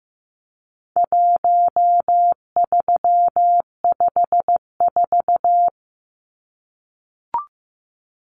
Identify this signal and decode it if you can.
Morse code, heard as a keyed tone, about 15 wpm, 700 Hz. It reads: 1354
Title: Numbers - 4-Digit 15wpm